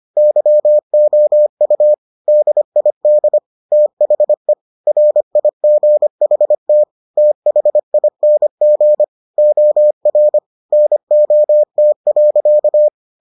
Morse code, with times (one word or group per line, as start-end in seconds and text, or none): 0.2-1.9 YOU
2.3-3.4 DID
3.7-4.5 THE
4.9-6.8 RIGHT
7.2-9.0 THING
9.4-10.4 OR
10.7-12.9 NOT.